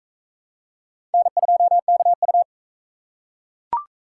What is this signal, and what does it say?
Morse code, heard as a keyed tone, about 37 wpm, 700 Hz. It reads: N2KU